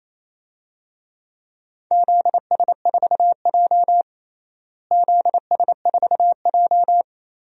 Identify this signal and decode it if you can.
Morse code, heard as a keyed tone, about 28 wpm, 700 Hz. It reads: ZS4J ZS4J
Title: DXCC-100 Call Signs - 4 Characters - Repeated 3x - 28wpm